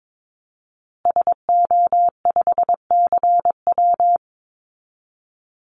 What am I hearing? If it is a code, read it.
Morse code, heard as a keyed tone, about 22 wpm, 700 Hz. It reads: SO5CW
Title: Call Signs Repeated 3x - 22wpm